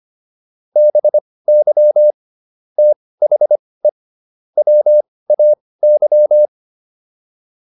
Morse code, heard as a keyed tone, about 25 wpm, spaced as for 18 wpm, 600 Hz. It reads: BY THE WAY